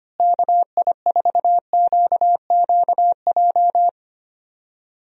Morse code, heard as a keyed tone, about 25 wpm, 700 Hz. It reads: KI4QQJ
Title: USA Call Signs - 6 Characters - Repeated 3x - 25wpm